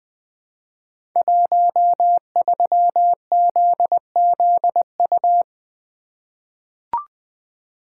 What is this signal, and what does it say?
Morse code, heard as a keyed tone, about 20 wpm, 700 Hz. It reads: 13ZZU
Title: Code Groups - 5 Letter:Numbers - 20wpm